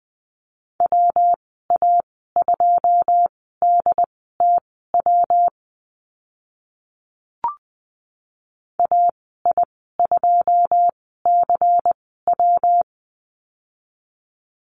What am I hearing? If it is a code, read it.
Morse code, heard as a keyed tone, about 20 wpm, spaced as for 14 wpm, 700 Hz. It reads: WA2DTW AI2CW
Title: 122 Lesson 20g - Number 3 - Call Signs 20wpm